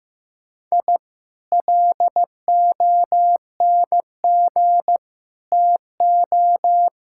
Morse code, heard as a keyed tone, about 15 wpm, 700 Hz. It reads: I LONG TO
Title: Sentences from Top 1000 Words - Review Part 6 - 15wpm